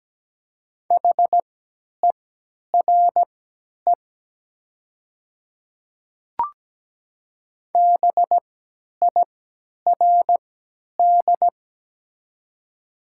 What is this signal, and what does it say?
Morse code, heard as a keyed tone, about 17 wpm, spaced as for 10 wpm, 700 Hz. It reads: HERE BIRD
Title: 3x Character Spacing - Top 500 Words - 17wpm